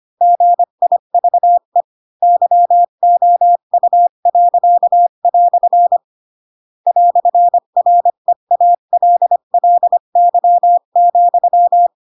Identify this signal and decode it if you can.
Morse code, heard as a keyed tone, about 25 wpm, 700 Hz. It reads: GIVE YOU." "REALLY,
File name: Pinocchio / Pinocchio ch7 20 25 ik1hge